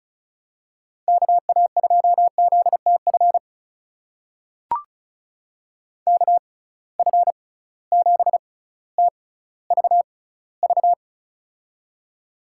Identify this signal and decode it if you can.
Morse code, heard as a keyed tone, about 34 wpm, spaced as for 18 wpm, 700 Hz. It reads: KA2ZTF KF7TVV
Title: USA Call Signs - 6 Characters - 6x Character Spacing - 35wpm